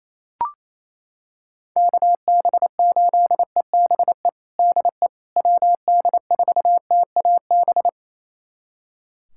Morse code, heard as a keyed tone, about 28 wpm, 700 Hz. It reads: KB8EBE DE WD4TAB
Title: USA Callsign DE Callsign - 6 Characters - 28wpm